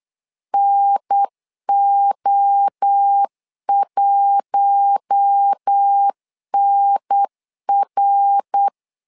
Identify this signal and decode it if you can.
Morse code, heard as a keyed tone, about 8 wpm, 800 Hz. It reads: NO1NR
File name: LEZIONI / 4 Q1 8wpm 3035